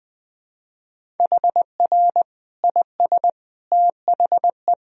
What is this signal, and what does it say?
Morse code, heard as a keyed tone, about 20 wpm, 700 Hz. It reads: HR IS THE
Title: Sentences from Top 300 Words and Abbreviations 20wpm